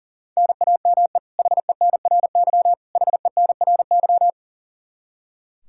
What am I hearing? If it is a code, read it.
Morse code, heard as a keyed tone, about 40 wpm, 700 Hz. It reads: NAME HENRY HENRY
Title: Practice QSO - Exchange 1 - Beginning - 40wpm